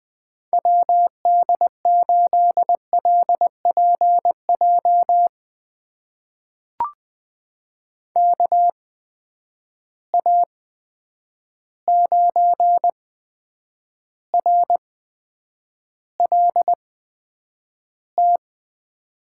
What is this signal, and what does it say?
Morse code, heard as a keyed tone, about 20 wpm, spaced as for 9 wpm, 700 Hz. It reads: WD8LPJ KA9RLT
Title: USA Call Signs - 6 Characters - 8x Character Spacing - 20wpm